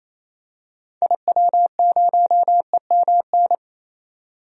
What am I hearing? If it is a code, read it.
Morse code, heard as a keyed tone, about 28 wpm, 700 Hz. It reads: IW0EMN
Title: DXCC-100 Call Signs - 6 Characters - 28wpm